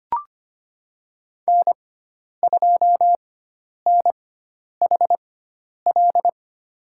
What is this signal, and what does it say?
Morse code, heard as a keyed tone, about 25 wpm, spaced as for 10 wpm, 700 Hz. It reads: N2NHL
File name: Lesson 16g - Letter c - Call Signs 25wpm